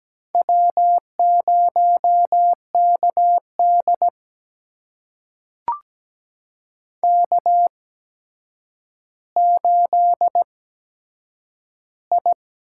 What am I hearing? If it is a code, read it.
Morse code, heard as a keyed tone, about 17 wpm, spaced as for 8 wpm, 700 Hz. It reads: W0KD K8I
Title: USA Call Signs - 4 Characters - 8x Character Spacing - 17wpm